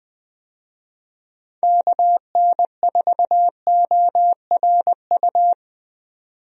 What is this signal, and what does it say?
Morse code, heard as a keyed tone, about 20 wpm, 700 Hz. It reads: KN4ORU
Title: USA Call Signs - 6 Characters - 20wpm